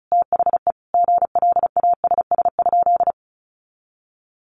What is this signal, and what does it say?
Morse code, heard as a keyed tone, about 34 wpm, 700 Hz. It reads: THE GLASS?
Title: Sentences from Top 1300 Words 35wpm